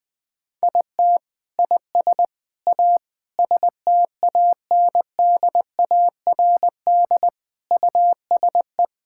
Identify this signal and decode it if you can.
Morse code, heard as a keyed tone, about 21 wpm, 700 Hz. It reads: IT IS A STANDARD USE
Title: Sentences from Top 1000 Words - Review Part 6 - 20wpm